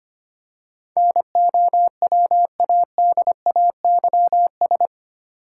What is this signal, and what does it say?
Morse code, heard as a keyed tone, about 25 wpm, 690 Hz. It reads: NOWADAYS
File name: 8 Letter Words - First 3 Letters Given - 25wpm